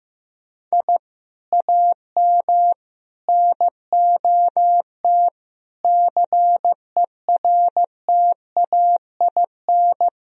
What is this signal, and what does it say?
Morse code, heard as a keyed tone, about 15 wpm, 700 Hz. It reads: I AM NOT CERTAIN
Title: Sentences from Top 1000 Words - Review Part 2 - 15wpm